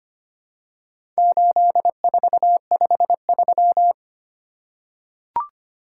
Morse code, heard as a keyed tone, about 25 wpm, 690 Hz. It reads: 8453
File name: practice / 25wpm / Four Letter-Number Combinations - 25wpm